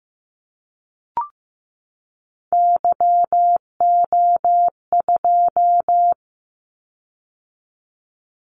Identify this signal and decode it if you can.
Morse code, heard as a keyed tone, about 15 wpm, 700 Hz. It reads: YO2
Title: Callsign Prefixes for Top 25 DXCC - 15wpm